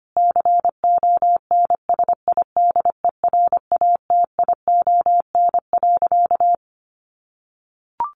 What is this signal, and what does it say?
Morse code, heard as a keyed tone, about 25 wpm, 700 Hz. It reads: CONSIDERATION.
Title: Sentences from Top 1500 Words 25wpm